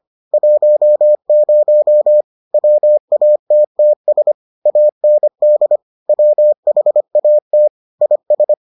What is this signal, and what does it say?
Morse code, heard as a keyed tone, about 25 wpm, 600 Hz. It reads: 10 WATTS AND WHAT IS